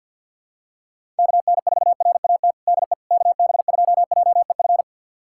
Morse code, heard as a keyed tone, about 49 wpm, 700 Hz. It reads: KN4RAT DE KB2JEF
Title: USA Callsign DE Callsign - 6 Characters - 50wpm